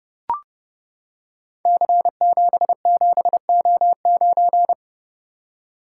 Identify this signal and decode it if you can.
Morse code, heard as a keyed tone, about 30 wpm, 700 Hz. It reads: C77O9
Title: Code Groups - 5 Letter:Numbers - 30wpm